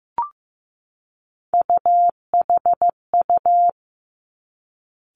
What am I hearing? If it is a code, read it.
Morse code, heard as a keyed tone, about 15 wpm, 700 Hz. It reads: UHU
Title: Three Challenging Letter-Number Combinations 15wpm